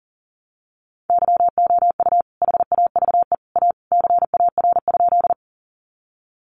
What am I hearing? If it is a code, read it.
Morse code, heard as a keyed tone, about 37 wpm, 700 Hz. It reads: YOU HAVE A CAR?
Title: Sentences from Top 300 Words 40wpm